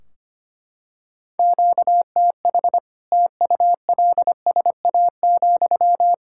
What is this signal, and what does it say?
Morse code, heard as a keyed tone, about 25 wpm, 700 Hz. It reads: QTH TULSA,
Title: Practice QSO - Exchange 1 - Beginning - 25wpm